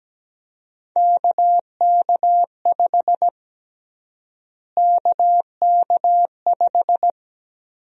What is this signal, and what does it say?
Morse code, heard as a keyed tone, about 17 wpm, 700 Hz. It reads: KK5 KK5
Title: USA Call Sign Prefixes - Repeated 2x - 17wpm